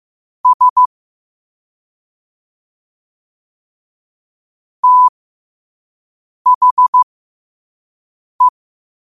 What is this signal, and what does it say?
Morse code, heard as a keyed tone, about 15 wpm, spaced as for 5 wpm, 1000 Hz. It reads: S THE
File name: qotdfiles / 5wpm